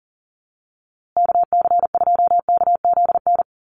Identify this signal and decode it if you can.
Morse code, heard as a keyed tone, about 38 wpm, 700 Hz. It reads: KC2KZN
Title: USA Call Signs Repeated 3x 40wpm